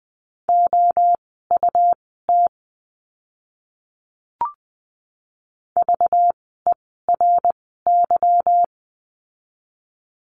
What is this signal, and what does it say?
Morse code, heard as a keyed tone, about 20 wpm, spaced as for 14 wpm, 700 Hz. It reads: OUT VERY
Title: 230 Lesson 37f - Letter x - Words 20wpm